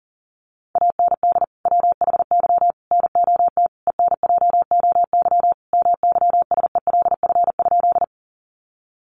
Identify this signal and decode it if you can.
Morse code, heard as a keyed tone, about 40 wpm, 700 Hz. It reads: AND WHY NOT ENJOY MYSELF?
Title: Sentences from Top 1300 Words 40wpm